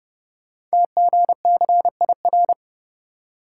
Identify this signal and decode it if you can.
Morse code, heard as a keyed tone, about 30 wpm, 690 Hz. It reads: TGCIR
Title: Code Groups - 5 Letters - 30wpm